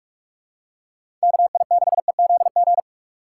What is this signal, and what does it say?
Morse code, heard as a keyed tone, about 42 wpm, 700 Hz. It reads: KI6EZG
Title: USA Call Signs - 6 Characters - 3x Character Spacing - 45wpm